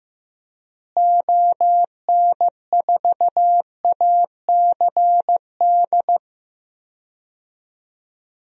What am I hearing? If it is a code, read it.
Morse code, heard as a keyed tone, about 15 wpm, 700 Hz. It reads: ON4ACD
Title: DXCC-100 Call Signs - 6 Characters - Repeated 2x - 15wpm